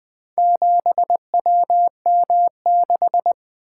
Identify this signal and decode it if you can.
Morse code, heard as a keyed tone, about 23 wpm, 700 Hz. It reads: 7WM6
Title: Speed-Racing - Four Letter-Number Combinations - 20wpm